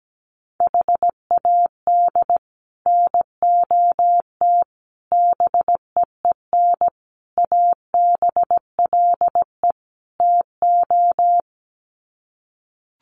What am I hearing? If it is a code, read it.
Morse code, heard as a keyed tone, about 17 wpm, 700 Hz. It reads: HAD NOT BEEN ABLE TO
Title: Sets of 5 Words Encore 17wpm